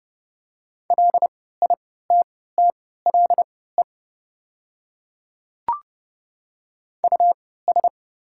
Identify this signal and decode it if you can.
Morse code, heard as a keyed tone, about 30 wpm, spaced as for 17 wpm, 700 Hz. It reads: LITTLE US
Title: 3x Character Spacing - Top 200 Words - 30wpm